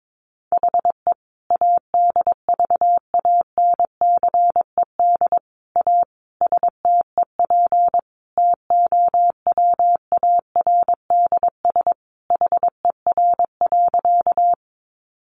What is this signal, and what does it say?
Morse code, heard as a keyed tone, about 22 wpm, 700 Hz. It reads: HE ADVANCED A STEP TOWARDS HER.
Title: Sentences from Top 1200 Words 22wpm